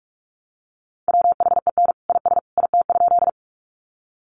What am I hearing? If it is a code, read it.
Morse code, heard as a keyed tone, about 44 wpm, 700 Hz. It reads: WHEN IS IT?
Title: Sentences from Top 100 Words 45wpm